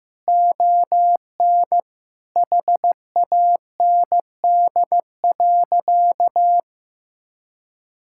Morse code, heard as a keyed tone, about 15 wpm, 700 Hz. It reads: ON HAND.
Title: Sentences from Top 1000 Words - Review Part 5 - 15wpm